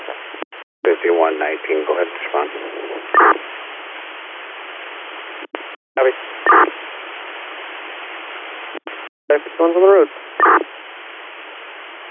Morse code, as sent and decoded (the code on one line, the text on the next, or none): none